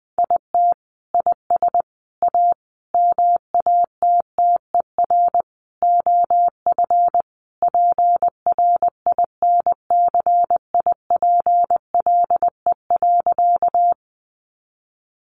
Morse code, heard as a keyed tone, about 20 wpm, 700 Hz. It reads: IT IS A MATTER OF PRINCIPLE.
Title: Sentences from Top 1400 Words 20wpm